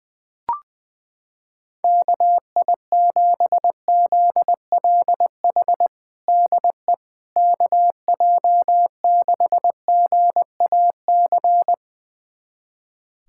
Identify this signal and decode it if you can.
Morse code, heard as a keyed tone, about 20 wpm, 700 Hz. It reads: KI7ZLH DE KJ6GAC
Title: USA Callsign DE Callsign - 6 Characters - 20wpm